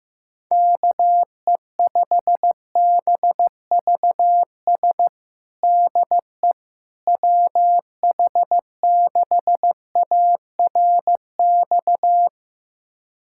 Morse code, heard as a keyed tone, about 15 wpm, 700 Hz. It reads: KE5BVS DE WH6ARX
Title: USA Callsign DE Callsign - 6 Characters - 15wpm